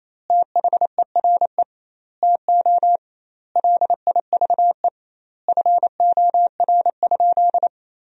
Morse code, heard as a keyed tone, about 28 wpm, 700 Hz. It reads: THERE TO LIVE FOR?
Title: 2x Word-Spacing - Sentences from Top 200 Words - 28wpm